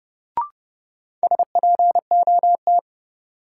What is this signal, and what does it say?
Morse code, heard as a keyed tone, about 32 wpm, 700 Hz. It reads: SPOT
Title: Sentences from Top 1100 Words - New Word Drill Alt - 30wpm